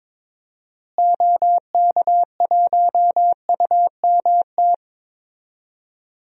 Morse code, heard as a keyed tone, about 22 wpm, 700 Hz. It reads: OK1UMT
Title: DXCC-100 Call Signs - 6 Characters - Repeated 3x - 22wpm